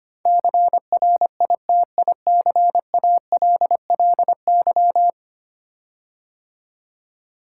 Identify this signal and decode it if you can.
Morse code, heard as a keyed tone, about 25 wpm, 700 Hz. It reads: CRITICALLY
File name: 10 Letter Words - First 3 Letters Given - 25wpm